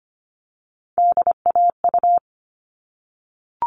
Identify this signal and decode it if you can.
Morse code, heard as a keyed tone, about 25 wpm, 700 Hz. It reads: DAU
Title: All Cut Numbers - 3-Digit 25wpm